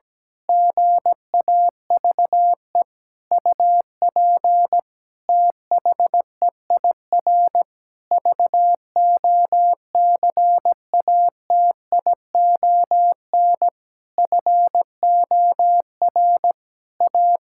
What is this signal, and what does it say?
Morse code, heard as a keyed tone, about 17 wpm, 700 Hz. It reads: GAVE UP THEIR VOCATION FOR A
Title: Pangram Challenge v2 - 17wpm